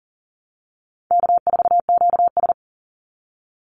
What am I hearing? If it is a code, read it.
Morse code, heard as a keyed tone, about 40 wpm, 700 Hz. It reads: K4QS
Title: NA SOTA Activator Callsigns 40wpm